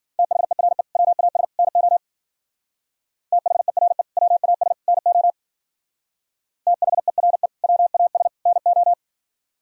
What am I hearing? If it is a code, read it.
Morse code, heard as a keyed tone, about 50 wpm, 700 Hz. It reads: THERE WAS NO THERE WAS NO THERE WAS NO
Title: Speed-Racing - Sets of 3 Words 40wpm